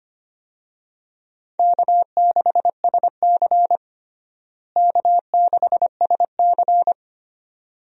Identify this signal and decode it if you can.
Morse code, heard as a keyed tone, about 25 wpm, 700 Hz. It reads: K6SC K6SC
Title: USA Call Signs - 4 Characters - Repeated 2x - 25wpm